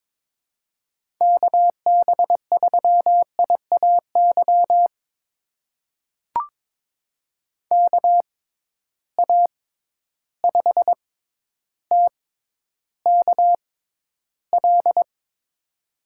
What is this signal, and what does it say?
Morse code, heard as a keyed tone, about 22 wpm, spaced as for 11 wpm, 700 Hz. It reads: KB3IAY KA5TKL
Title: USA Call Signs - 6 Characters - 6x Character Spacing - 22wpm